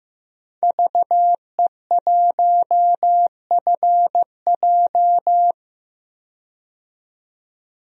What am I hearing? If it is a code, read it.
Morse code, heard as a keyed tone, about 15 wpm, 700 Hz. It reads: VE1FJ
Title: DXCC-100 Call Signs - 5 Characters - 15wpm